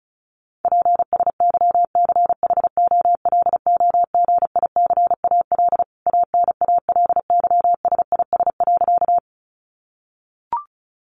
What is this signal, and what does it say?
Morse code, heard as a keyed tone, about 34 wpm, 700 Hz. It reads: PSYCHOLOGICAL ANALYSIS.
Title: Sentences from Top 1400 Words 35wpm